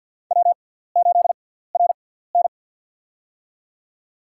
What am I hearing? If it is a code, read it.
Morse code, heard as a keyed tone, about 47 wpm, spaced as for 17 wpm, 700 Hz. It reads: W8RN
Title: USA Call Signs - 6x Character Spacing - 50wpm